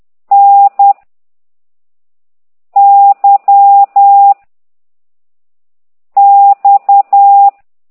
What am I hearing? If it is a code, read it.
Morse code, heard as a keyed tone, about 10 wpm, 800 Hz. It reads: N Y X
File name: ct2idw / exerc4